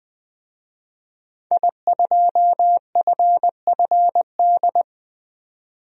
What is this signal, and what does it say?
Morse code, heard as a keyed tone, about 20 wpm, 700 Hz. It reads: I2FFD
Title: DXCC-100 Call Signs - 5 Characters - 20wpm